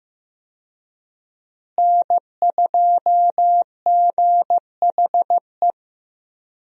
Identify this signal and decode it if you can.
Morse code, heard as a keyed tone, about 15 wpm, 700 Hz. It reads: N2GHE
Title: USA Call Signs - 5 Characters - Repeated 3x - 15wpm